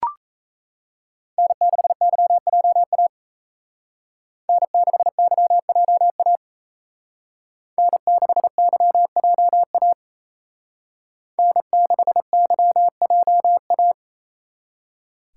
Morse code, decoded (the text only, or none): N6YJA N6YJA N6YJA N6YJA